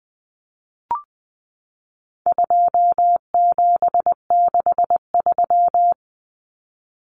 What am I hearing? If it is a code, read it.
Morse code, heard as a keyed tone, about 20 wpm, 700 Hz. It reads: 2763
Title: Common Cut Numbers - 4-Digit 20wpm